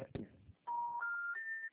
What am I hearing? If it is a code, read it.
Special information tone, with speech.